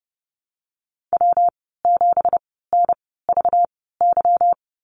Morse code, heard as a keyed tone, about 30 wpm, spaced as for 17 wpm, 700 Hz. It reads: W7NVY
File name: Lesson 37g - Letter x - Call Signs 30wpm